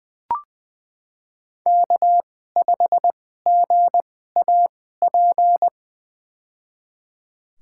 Morse code, heard as a keyed tone, about 21 wpm, spaced as for 15 wpm, 700 Hz. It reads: K5GAP
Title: USA Call Signs - 2x Character Spacing - 20wpm